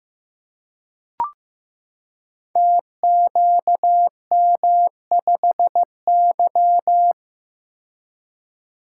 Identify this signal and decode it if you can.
Morse code, heard as a keyed tone, about 15 wpm, 700 Hz. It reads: TQM5Y
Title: Code Groups - 5 Letter:Numbers - 15wpm